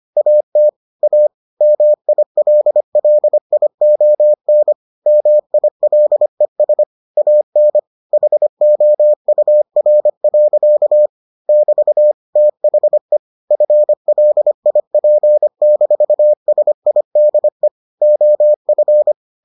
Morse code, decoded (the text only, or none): AT A MILLION MILES AN HOUR. <BT> THE FLIP-SIDE OF